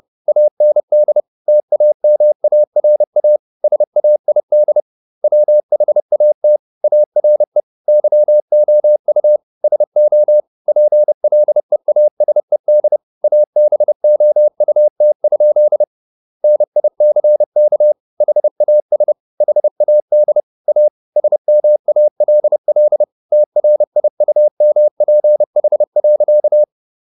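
Morse code, heard as a keyed tone, about 30 wpm, 600 Hz. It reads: AND TAMARA SAID WHAT ARE YOU SO PLEASED ABOUT? NICK HAS HAD A SMALL TRIUMPH.
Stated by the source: Morse code at 30 wpm, the novel Never by Ken Follett in English, from CW med Gnister.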